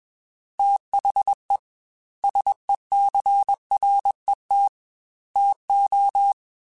Morse code, heard as a keyed tone, about 21 wpm, 800 Hz. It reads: THE SECRET TO
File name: Technical Subjects / Technical Subjects-mp3s / Sw Pwr Sup @20